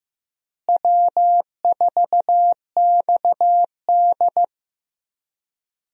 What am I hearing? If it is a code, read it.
Morse code, heard as a keyed tone, about 16 wpm, 700 Hz. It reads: W4XD
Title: USA Call Signs - 4 Characters - Repeated 3x - 15wpm